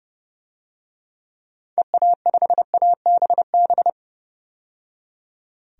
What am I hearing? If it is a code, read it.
Morse code, heard as a keyed tone, about 30 wpm, 700 Hz. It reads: EA5ABB
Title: DXCC-100 Call Signs - 6 Characters - 30wpm